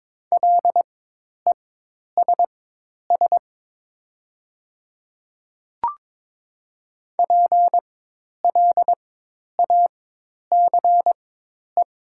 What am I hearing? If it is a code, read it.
Morse code, heard as a keyed tone, about 22 wpm, spaced as for 10 wpm, 700 Hz. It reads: LESS PLACE
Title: 4x Character Spacing - Top 500 Words - 22wpm